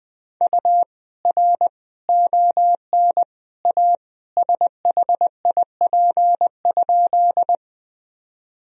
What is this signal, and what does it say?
Morse code, heard as a keyed tone, about 20 wpm, 700 Hz. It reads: U R ON A SHIP?
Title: Sentences from Top 400 Words and Abbreviations 20wpm